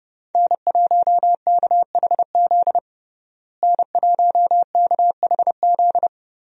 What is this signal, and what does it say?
Morse code, heard as a keyed tone, about 32 wpm, 700 Hz. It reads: N1KHZ N1KHZ
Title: USA Call Signs - 5 Characters - Repeated 2x - 30wpm